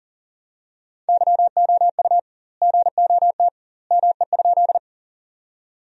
Morse code, heard as a keyed tone, about 40 wpm, 700 Hz. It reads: YOU GOT ME?
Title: 2x Word-Spacing - Sentences from Top 300 Words - 40wpm